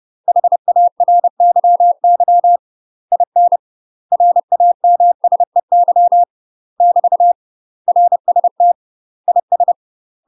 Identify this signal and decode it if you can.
Morse code, heard as a keyed tone, about 30 wpm, 700 Hz. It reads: HARYY IN RAMSEY <BT> RST IS
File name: UK MASTER QSOs TRAINING / 30WPM / UK MASTER QSOs 0043